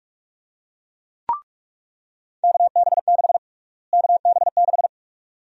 Morse code, heard as a keyed tone, about 41 wpm, 700 Hz. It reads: KB6 KB6
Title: USA Call Sign Prefixes - Repeated 2x - 45wpm